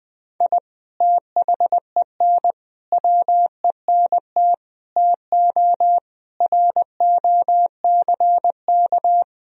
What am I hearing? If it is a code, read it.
Morse code, heard as a keyed tone, about 20 wpm, 700 Hz. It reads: I THEN WENT TO ROCK
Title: Sentences from Top 1000 Words - Review Part 3 - 20wpm